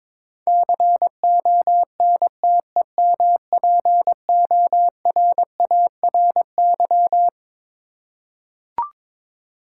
Morse code, heard as a keyed tone, about 22 wpm, 700 Hz. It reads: CONTEMPORARY
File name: Twelve Letter Words - 22wpm